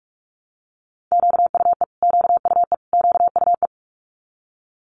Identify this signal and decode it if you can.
Morse code, heard as a keyed tone, about 46 wpm, 700 Hz. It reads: QUE QUE QUE
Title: Multi-letter Phonemes Repeated 3x 45wpm